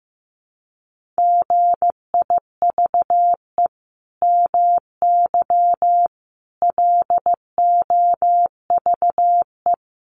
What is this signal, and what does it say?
Morse code, heard as a keyed tone, about 15 wpm, 700 Hz. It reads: GIVE MY LOVE
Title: Sentences from Top 1400 Words 15wpm